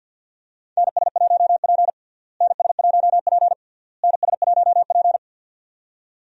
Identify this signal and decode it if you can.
Morse code, heard as a keyed tone, about 52 wpm, 700 Hz. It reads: NS1P NS1P NS1P
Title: USA Call Signs - 4 Characters - Repeated 3x - 50wpm